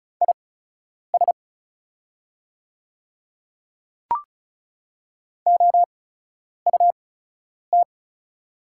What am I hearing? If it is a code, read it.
Morse code, heard as a keyed tone, about 36 wpm, spaced as for 10 wpm, 690 Hz. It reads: IS OUT
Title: 8x Character Spacing - Top 200 Words - 35wpm